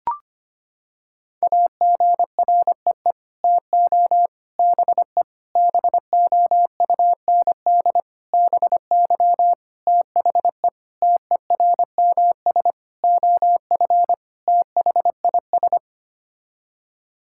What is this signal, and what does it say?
Morse code, heard as a keyed tone, about 25 wpm, 700 Hz. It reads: AGREE TO BE BOUND BY THE TERMS OF THIS
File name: Sets of 9 Words - 25wpm